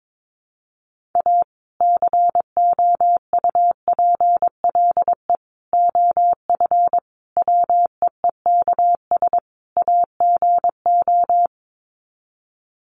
Morse code, heard as a keyed tone, about 22 wpm, 700 Hz. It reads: A COUPLE OF WEEKS AGO
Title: Sets of 5 Words 22wpm